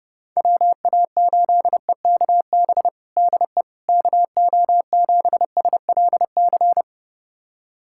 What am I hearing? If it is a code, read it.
Morse code, heard as a keyed tone, about 30 wpm, 700 Hz. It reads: WA8EKB DE KO7SLC
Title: USA Callsign DE Callsign - 6 Characters - 30wpm